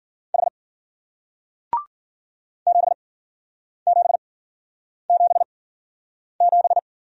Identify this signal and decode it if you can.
Morse code, heard as a keyed tone, about 48 wpm, spaced as for 9 wpm, 700 Hz. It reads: H 7777
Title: Speed-Racing Single Letter-Number 40wpm